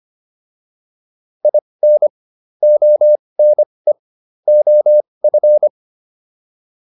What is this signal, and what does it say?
Morse code, heard as a keyed tone, about 25 wpm, spaced as for 20 wpm, 600 Hz. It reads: IN ONE OF
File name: WD202 20